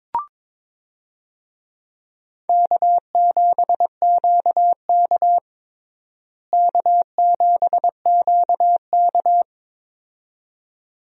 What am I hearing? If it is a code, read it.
Morse code, heard as a keyed tone, about 22 wpm, 700 Hz. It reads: K7QK K7QK
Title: USA Call Signs - 4 Characters - Repeated 2x - 22wpm